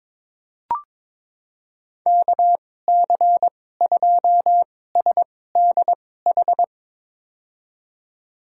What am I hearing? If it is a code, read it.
Morse code, heard as a keyed tone, about 22 wpm, spaced as for 16 wpm, 700 Hz. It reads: KC2SDH
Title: USA Call Signs - 6 Characters - 2x Character Spacing - 22wpm